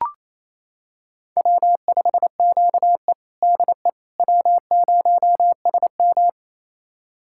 Morse code, heard as a keyed tone, about 28 wpm, 700 Hz. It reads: W5QE DE W0SM
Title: USA Callsign DE Callsign - 4 Characters - 28wpm